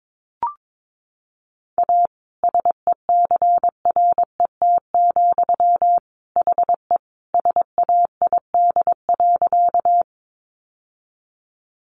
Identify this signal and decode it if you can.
Morse code, heard as a keyed tone, about 22 wpm, 700 Hz. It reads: A SECRET, HE SAID.
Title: Sentences from Top 1200 Words 22wpm